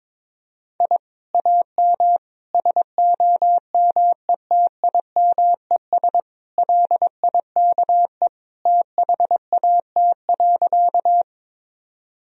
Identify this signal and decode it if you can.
Morse code, heard as a keyed tone, about 21 wpm, 700 Hz. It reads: I AM SOMETIMES LIKE THAT.
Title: Sentences from Top 1000 Words - Review Part 3 - 22wpm